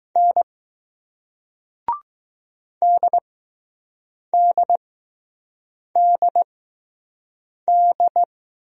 Morse code, heard as a keyed tone, about 20 wpm, spaced as for 6 wpm, 700 Hz. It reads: N DDDD